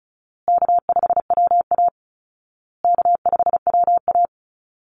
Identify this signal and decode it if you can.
Morse code, heard as a keyed tone, about 34 wpm, 700 Hz. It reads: K5WA K5WA
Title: Easy Call Signs Repeated 3x - 35wpm